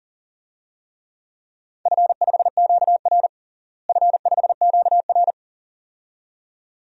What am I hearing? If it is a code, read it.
Morse code, heard as a keyed tone, about 40 wpm, 700 Hz. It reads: F5QR F5QR
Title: DXCC-100 Call Signs - 4 Characters - Repeated 2x - 40wpm